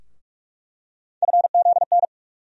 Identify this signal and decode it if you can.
Morse code, heard as a keyed tone, about 44 wpm, 700 Hz. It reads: FZN
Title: USA Call Sign Postfixes - 3 Letters 45wpm